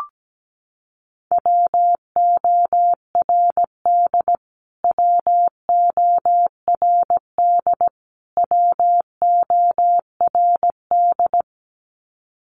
Morse code, heard as a keyed tone, about 17 wpm, 700 Hz. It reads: WORD WORD WORD
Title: Top 500 Words Repeated 17wpm